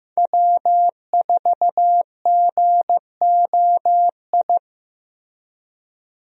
Morse code, heard as a keyed tone, about 15 wpm, 700 Hz. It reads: W4GOI
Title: USA Call Signs - 5 Characters - Repeated 3x - 15wpm